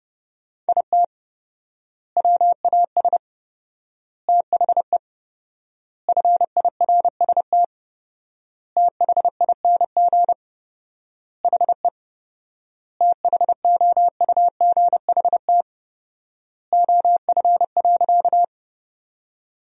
Morse code, heard as a keyed tone, about 32 wpm, 700 Hz. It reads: IT WAS THE FIRST THING HE THOUGHT OF.
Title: 4x Word-Spacing - Sentences from Top 300 Words - 30wpm